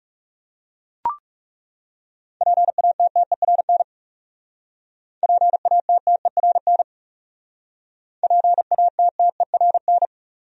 Morse code, heard as a keyed tone, about 40 wpm, 700 Hz. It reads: PATTERN PATTERN PATTERN